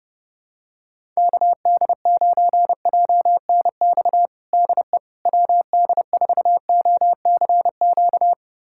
Morse code, heard as a keyed tone, about 32 wpm, 700 Hz. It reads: KD9JNX DE WD4OCQ
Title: USA Callsign DE Callsign 30wpm